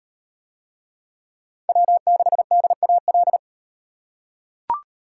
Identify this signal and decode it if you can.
Morse code, heard as a keyed tone, about 36 wpm, 700 Hz. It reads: W6DAL